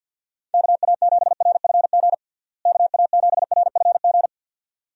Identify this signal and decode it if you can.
Morse code, heard as a keyed tone, about 49 wpm, 700 Hz. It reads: KA7RFG KA7RFG
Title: USA Call Signs - 6 Characters - Repeated 2x - 50wpm